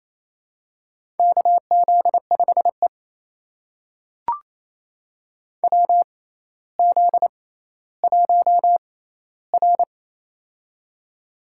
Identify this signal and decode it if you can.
Morse code, heard as a keyed tone, about 27 wpm, spaced as for 14 wpm, 700 Hz. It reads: KZ5E WZ1R
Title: USA Call Signs - 4 Characters - 6x Character Spacing - 28wpm